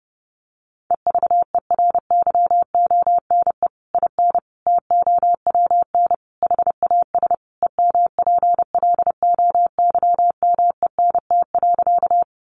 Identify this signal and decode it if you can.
Morse code, heard as a keyed tone, about 30 wpm, 700 Hz. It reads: EVERYONE IN TOWN HAS EMPLOYMENT.
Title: Sentences from Top 1500 Words 30wpm